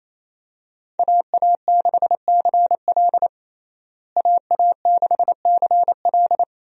Morse code, heard as a keyed tone, about 28 wpm, 700 Hz. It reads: AA6CL AA6CL
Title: USA Call Signs - 5 Characters - Repeated 2x - 28wpm